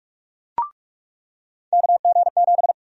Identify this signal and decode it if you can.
Morse code, heard as a keyed tone, about 46 wpm, 700 Hz. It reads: KG7
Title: USA Call Sign Prefixes - 45wpm